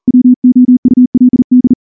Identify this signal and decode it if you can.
Morse code, heard as a keyed tone, about 38 wpm, 250 Hz. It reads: WOULD
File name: morse-training-40WPM